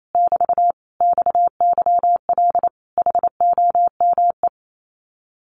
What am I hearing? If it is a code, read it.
Morse code, heard as a keyed tone, about 28 wpm, 700 Hz. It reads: <BT> XYL HOME
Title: Common QSO Phrases Encore 28wpm